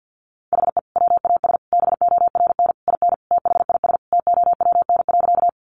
Morse code, heard as a keyed tone, about 49 wpm, 700 Hz. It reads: HE WAS BORN IN THIS TOWN.
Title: Sentences from Top 1000 Words 50wpm